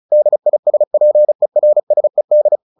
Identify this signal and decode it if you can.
Morse code, heard as a keyed tone, about 34 wpm, 600 Hz. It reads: DISPERSED
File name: the old man and the sea 2 35wpm